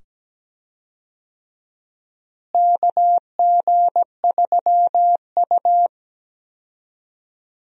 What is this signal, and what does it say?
Morse code, heard as a keyed tone, about 17 wpm, 700 Hz. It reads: KG3U
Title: USA Call Signs - 4 Characters - Repeated 2x - 17wpm